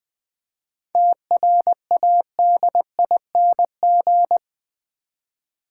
Morse code, heard as a keyed tone, about 20 wpm, 700 Hz. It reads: TRADING
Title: 7 Letter Words - First 3 Letters Given - 20wpm